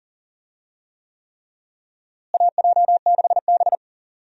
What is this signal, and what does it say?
Morse code, heard as a keyed tone, about 40 wpm, 700 Hz. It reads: AJ6B
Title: USA Call Signs - 4 Characters - Repeated 3x - 40wpm